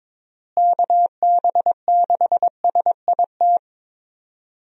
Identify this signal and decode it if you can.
Morse code, heard as a keyed tone, about 22 wpm, 700 Hz. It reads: KB6SIT
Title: USA Call Signs - 6 Characters - Repeated 3x - 22wpm